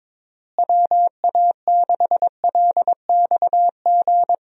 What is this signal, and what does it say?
Morse code, heard as a keyed tone, about 22 wpm, 700 Hz. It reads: WA6LXG
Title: USA Call Signs - 6 Characters - Repeated 3x - 22wpm